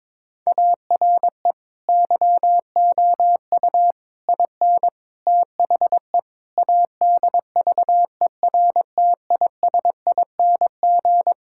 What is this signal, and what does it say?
Morse code, heard as a keyed tone, about 22 wpm, 700 Hz. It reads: ARE YOU IN THE ADVERTISING
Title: Sentences from Top 1000 Words - Review Part 6 - 22wpm